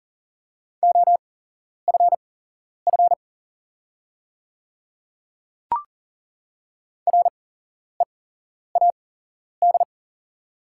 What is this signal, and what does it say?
Morse code, heard as a keyed tone, about 43 wpm, spaced as for 11 wpm, 700 Hz. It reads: OFF READ
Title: 8x Character Spacing - Top 200 Words - 40wpm